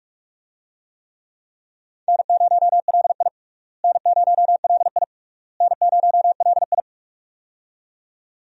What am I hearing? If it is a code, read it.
Morse code, heard as a keyed tone, about 43 wpm, 700 Hz. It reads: N0LI N0LI N0LI
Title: USA Call Signs - 4 Characters - Repeated 3x - 45wpm